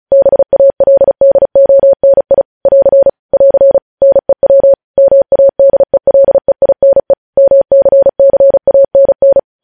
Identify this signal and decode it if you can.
Morse code, heard as a keyed tone, about 36 wpm, 550 Hz. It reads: BALDONI <AR> <AR> NEW MADELEINE MCCANN
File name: podcast / News headlines-1x-35x35